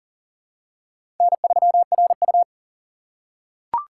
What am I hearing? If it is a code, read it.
Morse code, heard as a keyed tone, about 37 wpm, 700 Hz. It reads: N3RU
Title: USA Call Signs - 4 Characters - 4x Character Spacing - 40wpm